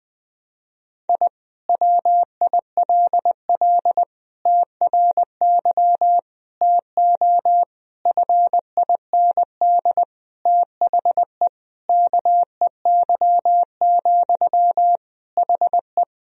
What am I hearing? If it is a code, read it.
Morse code, heard as a keyed tone, about 20 wpm, 700 Hz. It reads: I WILL TRY TO FIND THE KEY, HE
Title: Sentences from Top 1000 Words - Review Part 4 - 20wpm